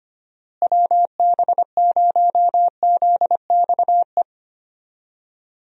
Morse code, heard as a keyed tone, about 25 wpm, 700 Hz. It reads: WB0ZXE
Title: USA Call Signs - 6 Characters - Repeated 3x - 25wpm